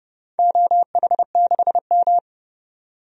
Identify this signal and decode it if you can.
Morse code, heard as a keyed tone, about 30 wpm, 700 Hz. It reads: OH6M
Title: DXCC-100 Call Signs - 4 Characters - 30wpm